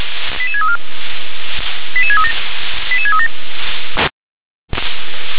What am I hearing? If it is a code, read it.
Five-tone selective call (ZVEI), heard as quick runs of five tones, tones of about 80 ms. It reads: 88634 88637 88637
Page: SDR/ZVEI